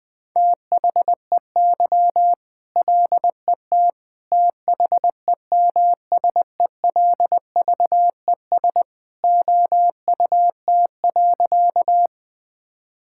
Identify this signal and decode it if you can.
Morse code, heard as a keyed tone, about 20 wpm, 700 Hz. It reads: THEY LET THEMSELVES OUT.
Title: Sentences from Top 1000 Words - Review Part 5 - 20wpm